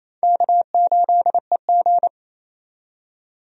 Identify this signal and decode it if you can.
Morse code, heard as a keyed tone, about 28 wpm, 700 Hz. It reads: K8EG
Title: USA Call Signs - 4 Characters - 28wpm